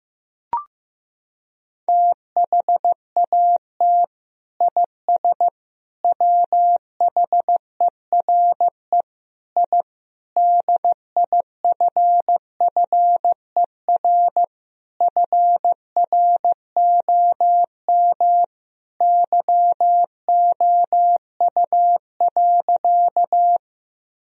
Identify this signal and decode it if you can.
Morse code, heard as a keyed tone, about 15 wpm, 700 Hz. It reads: THAT IS WHERE I DIFFER FROM YOU.
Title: Sentences from Top 1000 Words - Review Part 1 - 15wpm